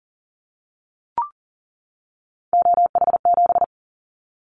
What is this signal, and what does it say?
Morse code, heard as a keyed tone, about 42 wpm, 700 Hz. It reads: OH7
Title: Callsign Prefixes for Top 50 DXCC - 40wpm